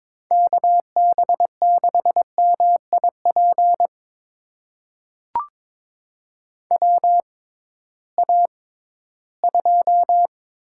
Morse code, heard as a keyed tone, about 22 wpm, spaced as for 14 wpm, 700 Hz. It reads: KB6MIP WA2
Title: USA Call Signs - 6 Characters - 6x Character Spacing - 22wpm